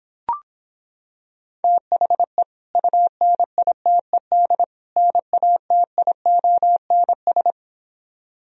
Morse code, heard as a keyed tone, about 26 wpm, 700 Hz. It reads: THE UNITED NATIONS